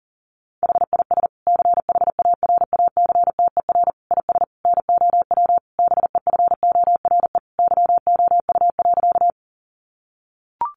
Morse code, heard as a keyed tone, about 40 wpm, 700 Hz. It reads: HIS CHARACTER IS NOW BEFORE YOU.
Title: Sentences from Top 1200 Words 40wpm